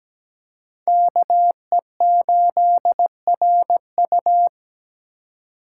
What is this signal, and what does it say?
Morse code, heard as a keyed tone, about 17 wpm, 700 Hz. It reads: KE8RU
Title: K1USN SST QSO Practice - 17wpm